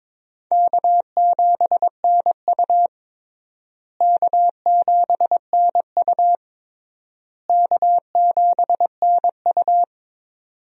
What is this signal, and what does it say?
Morse code, heard as a keyed tone, about 22 wpm, 700 Hz. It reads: K7NU K7NU K7NU
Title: USA Call Signs - 4 Characters - Repeated 3x - 22wpm